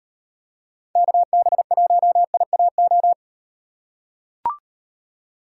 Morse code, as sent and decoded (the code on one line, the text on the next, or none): -.- -... .---- .. .- ---
KB1IAO